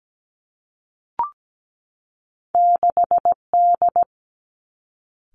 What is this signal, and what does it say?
Morse code, heard as a keyed tone, about 17 wpm, 700 Hz. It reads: 6 D
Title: Two Challenging Letter-Number Combinations 17wpm